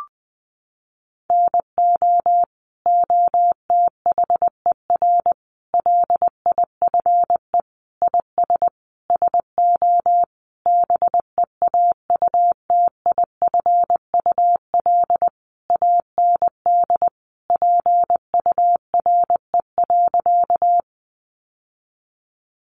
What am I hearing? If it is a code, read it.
Morse code, heard as a keyed tone, about 20 wpm, 700 Hz. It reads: NO OTHER LIFE IS SO BEAUTIFUL AND PURE.
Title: Sentences from Top 1500 Words 20wpm